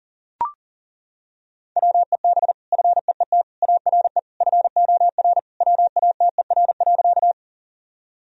Morse code, heard as a keyed tone, about 40 wpm, 700 Hz. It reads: WEB FEET ARE FOR WATER.
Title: Sentences from Top 1000 Words - Review Part 4 - 40wpm